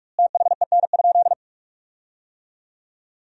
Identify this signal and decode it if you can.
Morse code, heard as a keyed tone, about 46 wpm, 700 Hz. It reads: THEN?
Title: 2x Word-Spacing - Sentences from Top 200 Words - 45wpm